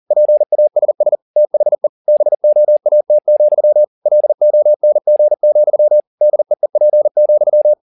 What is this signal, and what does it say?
Morse code, heard as a keyed tone, about 40 wpm, 600 Hz. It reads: PASS THE BOAT, LONG, DEEP,
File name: the old man and the sea 2 40wpm